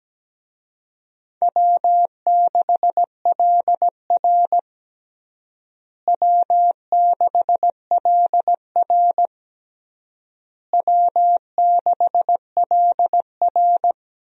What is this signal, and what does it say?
Morse code, heard as a keyed tone, about 17 wpm, 700 Hz. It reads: W6LR W6LR W6LR
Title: USA Call Signs - 4 Characters - Repeated 3x - 17wpm